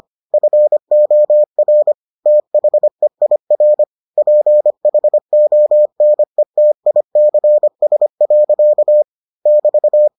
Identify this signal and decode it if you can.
Morse code, heard as a keyed tone, about 25 wpm, 600 Hz. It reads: FOR THEIR PHONETICS. <BT>